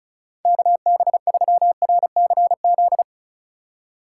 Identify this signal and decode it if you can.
Morse code, heard as a keyed tone, about 40 wpm, 700 Hz. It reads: KB3RCZ